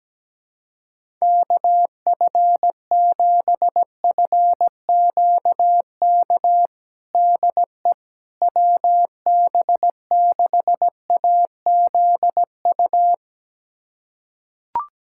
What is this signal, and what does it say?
Morse code, heard as a keyed tone, about 17 wpm, 700 Hz. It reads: KF7FQK DE WB6AZU
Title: USA Callsign DE Callsign - 6 Characters - 17wpm